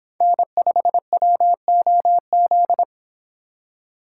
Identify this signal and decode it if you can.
Morse code, heard as a keyed tone, about 26 wpm, 700 Hz. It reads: N5WOZ